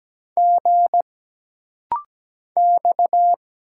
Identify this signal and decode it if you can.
Morse code, heard as a keyed tone, about 17 wpm, 700 Hz. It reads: G X
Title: Single Letters - Rapid-Fire 17wpm